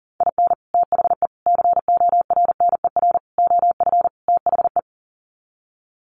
Morse code, heard as a keyed tone, about 40 wpm, 700 Hz. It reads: IN THE CORNER OF THE
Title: Sets of 5 Words Encore 40wpm